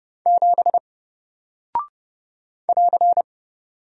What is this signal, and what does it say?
Morse code, heard as a keyed tone, about 30 wpm, 700 Hz. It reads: 7 <AR>
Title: Single Character (Letter:Number:Punctuation:Prosign) Alt 30wpm